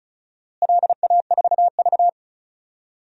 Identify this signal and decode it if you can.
Morse code, heard as a keyed tone, about 36 wpm, 700 Hz. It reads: LA4V
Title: DXCC-100 Call Signs - 4 Characters - Repeated 2x - 35wpm